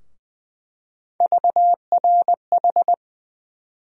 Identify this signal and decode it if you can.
Morse code, heard as a keyed tone, about 20 wpm, 700 Hz. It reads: VRH
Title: USA Call Sign Postfixes - 3 Letters 20wpm